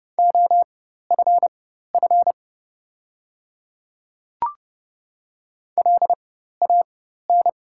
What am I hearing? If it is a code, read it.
Morse code, heard as a keyed tone, about 30 wpm, spaced as for 14 wpm, 700 Hz. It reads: OFF LAN
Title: 4x Character Spacing - Top 300 Words - 30wpm